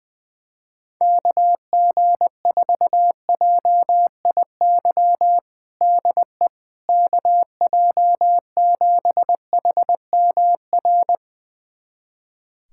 Morse code, heard as a keyed tone, about 20 wpm, 700 Hz. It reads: KG4JIY DE KJ7HMR
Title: USA Callsign DE Callsign 20wpm